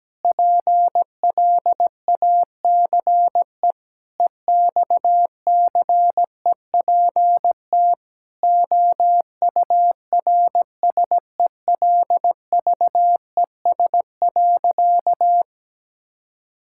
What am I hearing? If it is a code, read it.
Morse code, heard as a keyed tone, about 17 wpm, 700 Hz. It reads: PLACE EXCEPT OURSELVES.
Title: Sentences from Top 1600 Words 17wpm